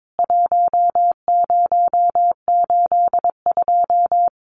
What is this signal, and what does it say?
Morse code, heard as a keyed tone, about 22 wpm, 700 Hz. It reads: 1082
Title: CWT Exchange 22wpm